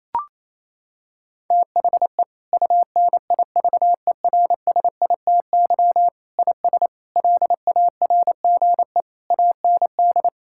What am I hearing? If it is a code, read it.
Morse code, heard as a keyed tone, about 28 wpm, 700 Hz. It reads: THE UNIVERSITY IS LARGE AND
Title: Sentences from Top 1000 Words - Review Part 5 - 28wpm